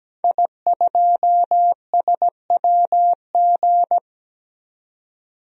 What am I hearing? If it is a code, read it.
Morse code, heard as a keyed tone, about 17 wpm, 700 Hz. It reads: I2SWG
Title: DXCC-100 Call Signs - 5 Characters - 17wpm